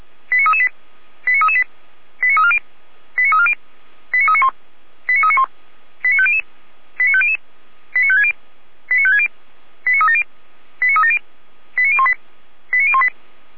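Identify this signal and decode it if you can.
Five-tone selective call (ZVEI), heard as quick runs of five tones, tones of about 70 ms. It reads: 89338 89338 89340 89340 89391 89391 89500 89500 89560 89560 89370 89370 89018 89018
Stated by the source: ZVEI five-tone call received by SDR, from feuerwehr-orlamuende.org (Feuerwehr Orlamünde) archive.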